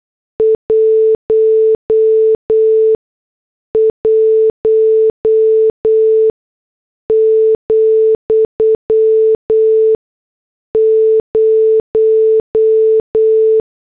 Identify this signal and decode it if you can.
Morse code, heard as a keyed tone, about 8 wpm, spaced as for 6 wpm, 440 Hz. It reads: 11,0